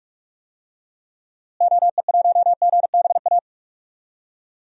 Morse code, heard as a keyed tone, about 46 wpm, 700 Hz. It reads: OE1GBA
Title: DXCC-100 Call Signs - 6 Characters - Repeated 3x - 45wpm